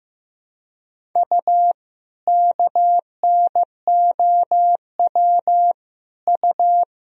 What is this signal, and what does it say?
Morse code, heard as a keyed tone, about 15 wpm, 700 Hz. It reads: U KNOW U
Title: Sentences from Top 500 Words and Abbreviations 15wpm